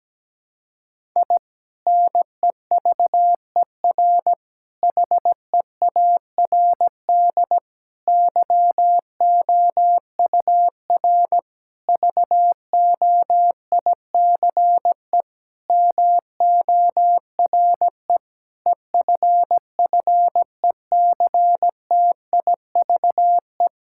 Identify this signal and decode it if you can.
Morse code, heard as a keyed tone, about 17 wpm, 700 Hz. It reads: I NEVER HEARD YOUR VOICE MORE EFFECTIVE
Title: Sentences from Top 1000 Words - Review Part 6 - 17wpm